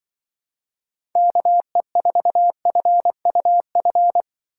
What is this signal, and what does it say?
Morse code, heard as a keyed tone, about 23 wpm, 700 Hz. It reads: KE4FUF
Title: Speed Racing - USA Call Signs 20wpm